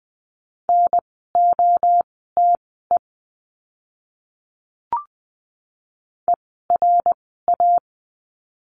Morse code, heard as a keyed tone, about 20 wpm, spaced as for 15 wpm, 700 Hz. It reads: NOTE ERA
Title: 054 Lesson 10f - Letter r - Words 20wpm